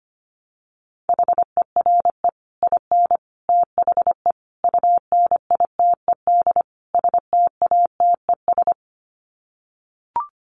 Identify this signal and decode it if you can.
Morse code, heard as a keyed tone, about 25 wpm, 700 Hz. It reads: HERE IN THE UNITED STATES
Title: Sets of 5 Words 25wpm